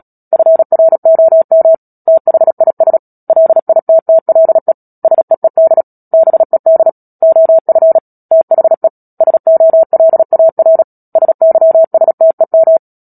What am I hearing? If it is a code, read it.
Morse code, heard as a keyed tone, about 37 wpm, 650 Hz. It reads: FROM THIS LITTLE SEED BED OF THE SOLAR SYSTEM
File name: Book Two / 10-Chapter 10 - Epilogue - 36 WPM